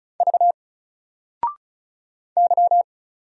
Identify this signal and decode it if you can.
Morse code, heard as a keyed tone, about 36 wpm, 700 Hz. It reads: V Y
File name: Single-Letter-Number-35wpm